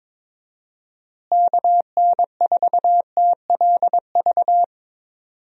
Morse code, heard as a keyed tone, about 22 wpm, 700 Hz. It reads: KN4TLV
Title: USA Call Signs - 6 Characters - Repeated 2x - 22wpm